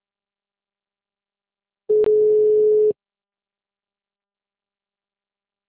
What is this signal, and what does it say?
Ringback tone (Europe, 425 Hz).
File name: out- -2007- - - .8027